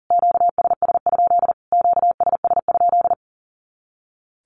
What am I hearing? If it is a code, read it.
Morse code, heard as a keyed tone, about 40 wpm, 700 Hz. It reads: QSS? QSS?
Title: Q-Codes Repeated 3x 40wpm